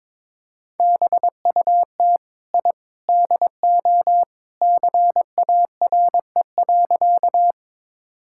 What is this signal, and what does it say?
Morse code, heard as a keyed tone, about 22 wpm, 700 Hz. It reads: BUT I DO CARE.
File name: practice / 22wpm / Sentences with 3-Word Phrases - 22wpm